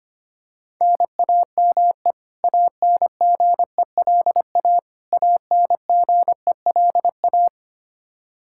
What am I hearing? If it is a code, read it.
Morse code, heard as a keyed tone, about 25 wpm, 700 Hz. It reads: NAME ANGELA ANGELA
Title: Practice QSO - Exchange 1 - Beginning - 25wpm